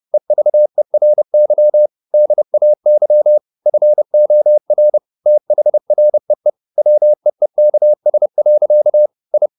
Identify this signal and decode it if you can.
Morse code, heard as a keyed tone, about 30 wpm, 600 Hz. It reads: EVERY DAY FOR THREE WEEKS. I
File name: the old man and the sea / the old man and the sea 1 30wpm